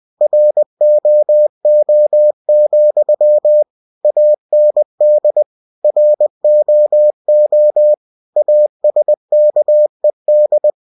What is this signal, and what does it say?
Morse code, heard as a keyed tone, about 20 wpm, 600 Hz. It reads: ROO, AND ROO ASKED
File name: cw-books / winnie-the-pooh / winnie-20wpm-page-0066